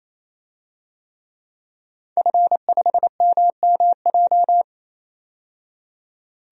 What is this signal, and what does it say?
Morse code, heard as a keyed tone, about 28 wpm, 700 Hz. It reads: F5MMJ
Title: DXCC-100 Call Signs - 5 Characters - 28wpm